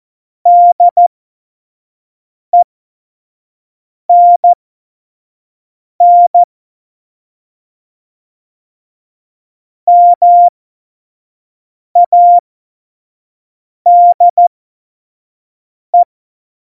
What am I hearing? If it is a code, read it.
Morse code, heard as a keyed tone, about 14 wpm, spaced as for 5 wpm, 700 Hz. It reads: DENN MADE
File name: Morsekurs / Witze / Witz 30. Juli 2019, 25 F70